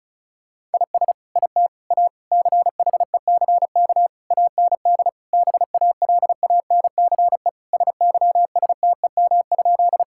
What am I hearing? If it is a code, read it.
Morse code, heard as a keyed tone, about 36 wpm, 700 Hz. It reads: IS IT A CHECK AND BALANCE SYSTEM?
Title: Sentences from Top 1000 Words - Review Part 6 - 35wpm